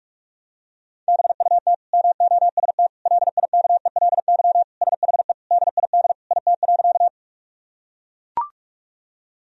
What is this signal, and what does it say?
Morse code, heard as a keyed tone, about 41 wpm, 700 Hz. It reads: BUT MOST LIKELY SHE DID IT.
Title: Sentences from Top 1000 Words - Review Part 4 - 45wpm